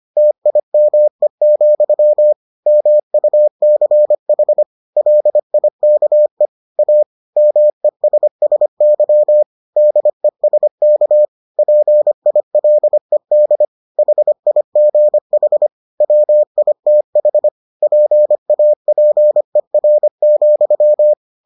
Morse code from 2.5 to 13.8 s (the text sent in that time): MUCH LIKE A MESSY DESK PILED